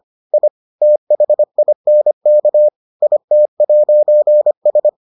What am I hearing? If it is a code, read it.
Morse code, heard as a keyed tone, about 25 wpm, 600 Hz. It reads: I THINK IT'S